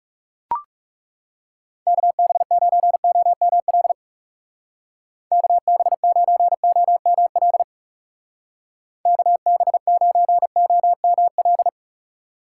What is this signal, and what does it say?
Morse code, heard as a keyed tone, about 37 wpm, 700 Hz. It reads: KB9OML KB9OML KB9OML
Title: Speed Racing - USA 2x3 Call Signs 30wpm